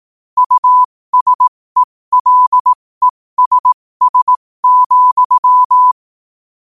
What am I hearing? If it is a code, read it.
Morse code, heard as a keyed tone, about 18 wpm, spaced as for 15 wpm, 1000 Hz. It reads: USELESS,
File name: qotdfiles / 15wpm